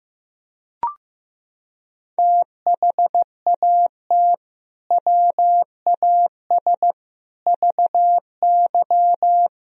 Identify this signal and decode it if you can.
Morse code, heard as a keyed tone, about 16 wpm, 700 Hz. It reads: THAT WAS VY
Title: Sentences from Top 200 Words and Abbreviations 15wpm